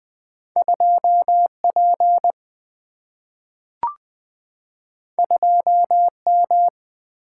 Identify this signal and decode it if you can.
Morse code, heard as a keyed tone, about 20 wpm, 700 Hz. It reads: 2P 2M
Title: Single-Letter Single-Number 20wpm